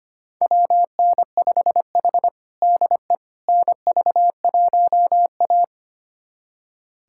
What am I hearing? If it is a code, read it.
Morse code, heard as a keyed tone, about 25 wpm, 700 Hz. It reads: WN5H DE NV1A
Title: USA Callsign DE Callsign - 4 Characters - 25wpm